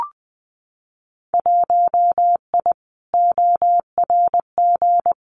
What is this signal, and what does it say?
Morse code, heard as a keyed tone, about 20 wpm, 700 Hz. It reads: 1I ORG
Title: Winter Field Day Exchanges Encore 20wpm